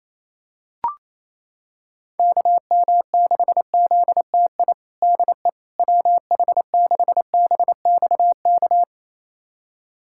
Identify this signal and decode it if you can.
Morse code, heard as a keyed tone, about 28 wpm, 700 Hz. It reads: KM6ZTI DE WH6BXK
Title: USA Callsign DE Callsign - 6 Characters - 28wpm